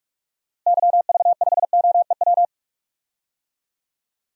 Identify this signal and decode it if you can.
Morse code, heard as a keyed tone, about 41 wpm, 700 Hz. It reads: YV5OEW
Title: DXCC-100 Call Signs - 6 Characters - Repeated 2x - 45wpm